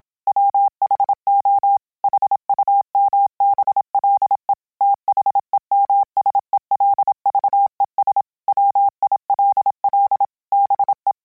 Morse code, heard as a keyed tone, about 27 wpm, 800 Hz. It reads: WHO HUMBLE THEMSELVES WILL BE